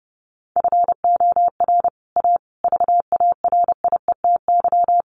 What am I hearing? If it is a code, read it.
Morse code, heard as a keyed tone, about 30 wpm, 700 Hz. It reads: FOR A VARIETY
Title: Sets of 5 Words Encore 30wpm